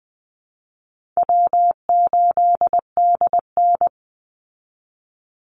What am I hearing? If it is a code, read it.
Morse code, heard as a keyed tone, about 20 wpm, 700 Hz. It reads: W8DN
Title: Easy Call Signs Repeated 3x - 20wpm